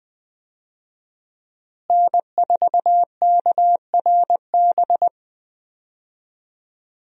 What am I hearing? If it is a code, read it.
Morse code, heard as a keyed tone, about 20 wpm, 700 Hz. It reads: N4KRB
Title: USA Call Signs - 5 Characters - Repeated 2x - 20wpm